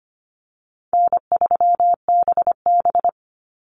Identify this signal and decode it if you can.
Morse code, heard as a keyed tone, about 25 wpm, 700 Hz. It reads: N3BB
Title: Easy Call Signs - 25wpm